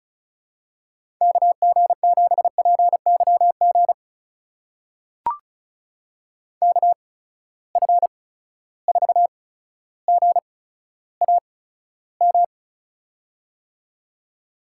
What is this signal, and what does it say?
Morse code, heard as a keyed tone, about 36 wpm, spaced as for 15 wpm, 700 Hz. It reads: KG7PYG KF4GAM
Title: USA Call Signs - 6 Characters - 8x Character Spacing - 35wpm